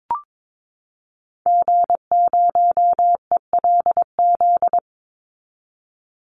Morse code, heard as a keyed tone, about 22 wpm, 700 Hz. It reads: G0ELZ
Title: Call Signs - 22wpm